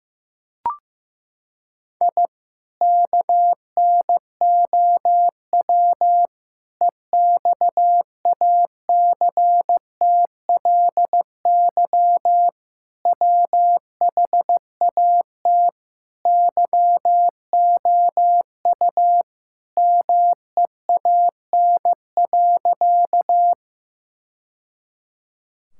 Morse code, heard as a keyed tone, about 15 wpm, 700 Hz. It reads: I KNOW EXACTLY WHAT YOU MEAN.
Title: Sentences from Top 1000 Words - Review Part 6 - 15wpm